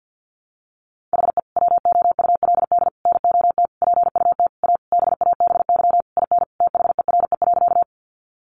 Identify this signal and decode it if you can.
Morse code, heard as a keyed tone, about 50 wpm, 700 Hz. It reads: HE WOULD NOT PUT A BABY IN THERE.
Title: Easy Phrases 50wpm